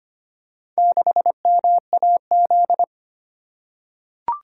Morse code, heard as a keyed tone, about 25 wpm, 700 Hz. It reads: 6MAZ
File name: practice / 25wpm / Four Letter-Number Combinations - 25wpm